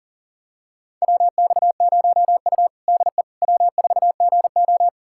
Morse code, heard as a keyed tone, about 40 wpm, 700 Hz. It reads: WX0U DE W4GO
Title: USA Callsign DE Callsign - 4 Characters - 40wpm